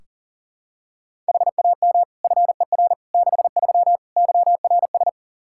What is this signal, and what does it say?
Morse code, heard as a keyed tone, about 40 wpm, 700 Hz. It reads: HAM FER 63 YRS
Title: Practice QSO - Exchange 3 - More Info - 40wpm